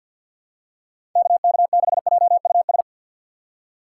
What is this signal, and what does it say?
Morse code, heard as a keyed tone, about 52 wpm, 700 Hz. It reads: KK6JUS
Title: USA Call Signs - 6 Characters - 4x Character Spacing - 50wpm